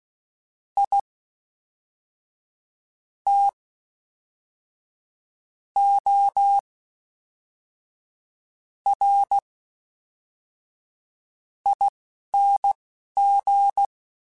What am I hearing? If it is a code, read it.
Morse code, heard as a keyed tone, about 16 wpm, spaced as for 5 wpm, 800 Hz. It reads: ITORING
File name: Special INGs / Special INGs-mp3s / Fast ing @10